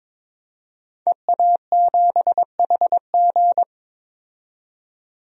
Morse code, heard as a keyed tone, about 22 wpm, 700 Hz. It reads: EA7HG
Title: DXCC-100 Call Signs - 5 Characters - 22wpm